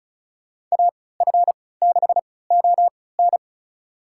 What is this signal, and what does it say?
Morse code, heard as a keyed tone, about 36 wpm, spaced as for 20 wpm, 700 Hz. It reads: AF6ON
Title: USA Call Signs - 5 Characters - 3x Character Spacing - 35wpm